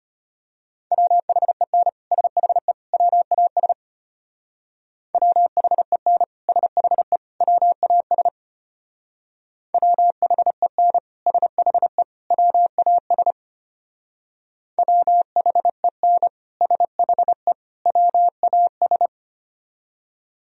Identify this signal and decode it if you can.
Morse code, heard as a keyed tone, about 32 wpm, 700 Hz. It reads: WHEN SHE WAS WHEN SHE WAS WHEN SHE WAS WHEN SHE WAS